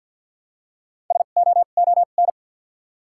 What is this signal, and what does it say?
Morse code, heard as a keyed tone, about 53 wpm, spaced as for 37 wpm, 700 Hz. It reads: SOON
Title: 2x Character Spacing - Top 500 Words - 50wpm